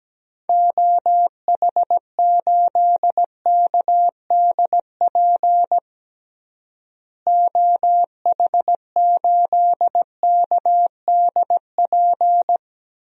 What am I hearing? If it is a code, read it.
Morse code, heard as a keyed tone, about 17 wpm, 700 Hz. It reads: OH8KDP OH8KDP
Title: DXCC-100 Call Signs - 6 Characters - Repeated 2x - 17wpm